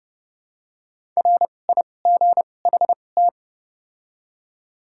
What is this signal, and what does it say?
Morse code, heard as a keyed tone, about 30 wpm, spaced as for 22 wpm, 700 Hz. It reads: RIGHT
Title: 2x Character Spacing - Top 200 Words - 30wpm